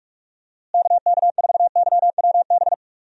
Morse code, heard as a keyed tone, about 46 wpm, 700 Hz. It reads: KK4YWB
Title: USA Call Signs - 6 Characters - Repeated 3x - 45wpm